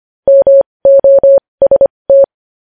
Morse code, heard as a keyed tone, about 25 wpm, spaced as for 20 wpm, 550 Hz. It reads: MOST
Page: news-headlines-in-morse-code-at-20-wpm